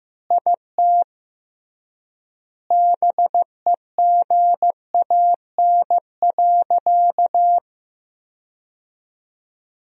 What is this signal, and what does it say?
Morse code, heard as a keyed tone, about 16 wpm, 700 Hz. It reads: IT BEGAN.
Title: 3x Word-Spacing - Sentences from Top 400 Words - 15wpm